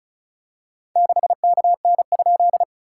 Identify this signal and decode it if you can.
Morse code, heard as a keyed tone, about 36 wpm, 690 Hz. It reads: 6KN?
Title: SOTA QSO Practice 35wpm